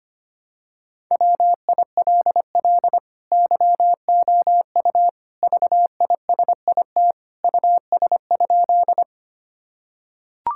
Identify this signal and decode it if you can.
Morse code, heard as a keyed tone, about 25 wpm, 700 Hz. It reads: WILL YOU VISIT US?
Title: Sentences from Top 1000 Words - Review Part 6 - 25wpm